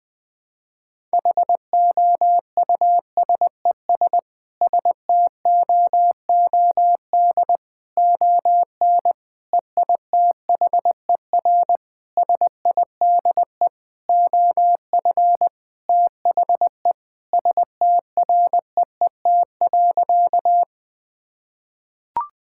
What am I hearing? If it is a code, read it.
Morse code, heard as a keyed tone, about 20 wpm, 700 Hz. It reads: HOUSES STOOD ON EITHER SIDE OF THE STREET.
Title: Sentences with 5-Word Phrases - 20wpm